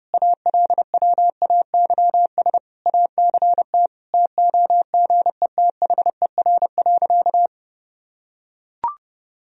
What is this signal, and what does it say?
Morse code, heard as a keyed tone, about 30 wpm, 700 Hz. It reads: ALWAYS ACT TOGETHER.
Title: Sentences from Top 1000 Words - Review Part 1 - 30wpm